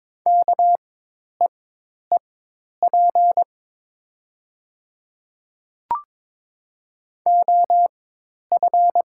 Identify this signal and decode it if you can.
Morse code, heard as a keyed tone, about 22 wpm, spaced as for 10 wpm, 700 Hz. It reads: KEEP OF
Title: 4x Character Spacing - Top 300 Words - 22wpm